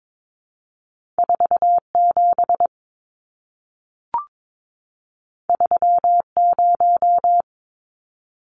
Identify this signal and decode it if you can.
Morse code, heard as a keyed tone, about 22 wpm, 700 Hz. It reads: 47 30
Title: Numbers - 2-Digit 22wpm